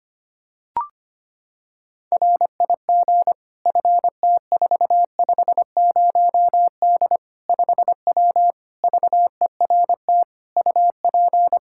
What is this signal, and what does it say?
Morse code, heard as a keyed tone, about 25 wpm, 700 Hz. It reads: RIG FT450D 5W VERT UP
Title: Practice QSO - Exchange 2 - Info - 25wpm